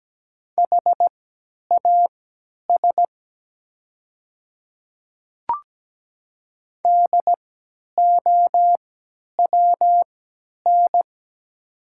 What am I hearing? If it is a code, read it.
Morse code, heard as a keyed tone, about 17 wpm, spaced as for 10 wpm, 700 Hz. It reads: HAS DOWN
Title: 3x Character Spacing - Top 100 Words - 17wpm